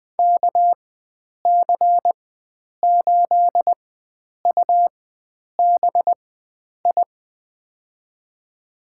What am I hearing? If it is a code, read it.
Morse code, heard as a keyed tone, about 20 wpm, spaced as for 9 wpm, 700 Hz. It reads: KC8UBI
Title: USA Call Signs - 4x Character Spacing - 20wpm